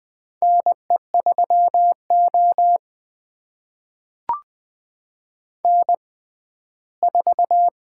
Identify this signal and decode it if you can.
Morse code, heard as a keyed tone, about 20 wpm, spaced as for 14 wpm, 700 Hz. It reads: NE3O N4
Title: USA Call Signs - 4 Characters - 6x Character Spacing - 20wpm